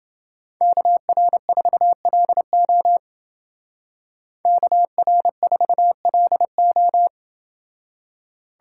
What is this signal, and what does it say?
Morse code, heard as a keyed tone, about 29 wpm, 700 Hz. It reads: KR4LO KR4LO